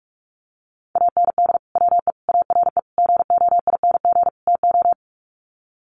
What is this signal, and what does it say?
Morse code, heard as a keyed tone, about 44 wpm, 700 Hz. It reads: AND WE ARE GOING TO
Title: Sets of 5 Words Encore2 45wpm